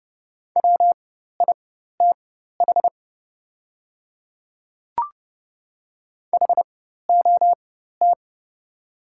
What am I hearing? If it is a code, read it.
Morse code, heard as a keyed tone, about 30 wpm, spaced as for 14 wpm, 700 Hz. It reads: WITH HOT
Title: 4x Character Spacing - Top 100 Words - 30wpm